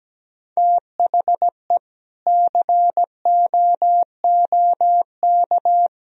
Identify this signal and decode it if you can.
Morse code, heard as a keyed tone, about 17 wpm, 700 Hz. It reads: THE COOK
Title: Sentences from Top 1000 Words - Review Part 4 - 17wpm